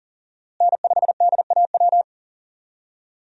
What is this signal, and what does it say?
Morse code, heard as a keyed tone, about 37 wpm, 700 Hz. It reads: N5DAW
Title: USA Call Signs - 5 Characters - Repeated 3x - 40wpm